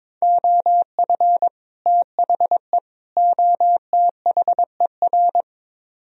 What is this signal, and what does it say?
Morse code, heard as a keyed tone, about 22 wpm, 700 Hz. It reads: OF THE OTHER
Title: Speed-Racing - Sets of 3 Words 22wpm